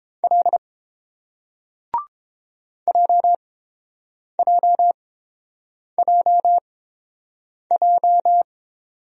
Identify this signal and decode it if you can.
Morse code, heard as a keyed tone, about 31 wpm, spaced as for 7 wpm, 700 Hz. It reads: L JJJJ